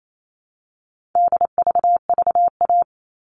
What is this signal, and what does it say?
Morse code, heard as a keyed tone, about 28 wpm, 700 Hz. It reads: DVVA
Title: All Cut Numbers - 4-Digit 28wpm